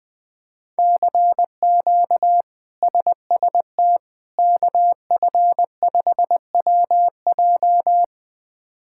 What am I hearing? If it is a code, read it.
Morse code, heard as a keyed tone, about 21 wpm, 700 Hz. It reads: CQ SST KF5WJ
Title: K1USN SST QSO Practice - 20wpm